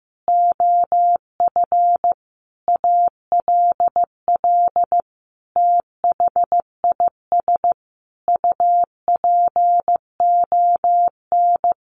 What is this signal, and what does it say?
Morse code, heard as a keyed tone, about 15 wpm, 700 Hz. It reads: OF ALL THIS UPON
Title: Sentences from Top 1400 Words 15wpm